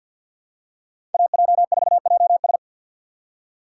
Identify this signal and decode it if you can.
Morse code, heard as a keyed tone, about 53 wpm, 700 Hz. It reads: AJ4JS
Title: USA Call Signs - 5 Characters - 3x Character Spacing - 50wpm